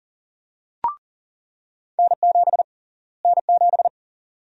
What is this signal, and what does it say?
Morse code, heard as a keyed tone, about 40 wpm, 700 Hz. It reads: N7 N7
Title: USA Call Sign Prefixes - Repeated 2x - 40wpm